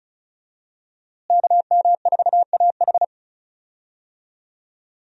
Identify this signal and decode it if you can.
Morse code, heard as a keyed tone, about 36 wpm, 700 Hz. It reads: KM4AH
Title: USA Call Signs - 5 Characters - Repeated 2x - 35wpm